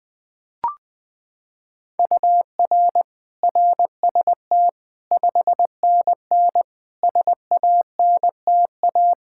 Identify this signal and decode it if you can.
Morse code, heard as a keyed tone, about 20 wpm, 700 Hz. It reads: UR RST 5NN SANTA
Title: Practice QSO - Exchange 1 - Beginning - 20wpm